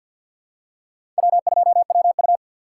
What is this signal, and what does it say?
Morse code, heard as a keyed tone, about 51 wpm, 690 Hz. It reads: W2WU
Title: USA Call Signs - 4x Character Spacing - 50wpm